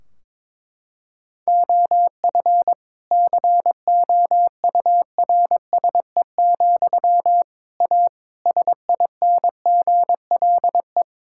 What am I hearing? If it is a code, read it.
Morse code, heard as a keyed tone, about 22 wpm, 700 Hz. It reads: OF COURSE, A SINGLE
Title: Sentences from Top 1600 Words 22wpm